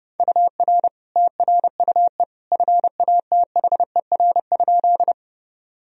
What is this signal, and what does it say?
Morse code, heard as a keyed tone, about 29 wpm, 700 Hz. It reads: UR TRUE FATHER?
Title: Sentences from Top 400 Words and Abbreviations 30wpm